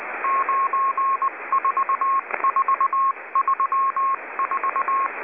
Morse code, heard as a keyed tone, about 19 wpm, 1080 Hz, with noise. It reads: 94434